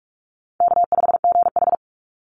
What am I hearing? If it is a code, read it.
Morse code, heard as a keyed tone, about 46 wpm, 700 Hz. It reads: K5GH
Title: Easy Call Signs - 45wpm